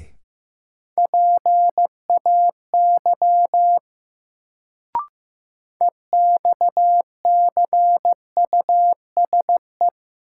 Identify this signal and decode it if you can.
Morse code, heard as a keyed tone, about 15 wpm, 700 Hz. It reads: PAY EXCUSE
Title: Sentences from Top 700 Words - New Word Drill - 15wpm